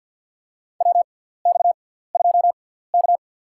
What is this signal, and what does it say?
Morse code, heard as a keyed tone, about 49 wpm, spaced as for 17 wpm, 700 Hz. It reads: WX2K
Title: USA Call Signs - 4 Characters - 6x Character Spacing - 50wpm